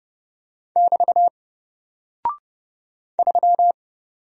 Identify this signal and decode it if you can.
Morse code, heard as a keyed tone, about 30 wpm, 700 Hz. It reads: <BT> 3
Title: Single Character (Letter:Number:Punctuation:Prosign) Alt 30wpm